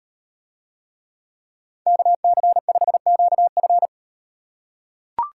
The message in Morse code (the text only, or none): KC5QF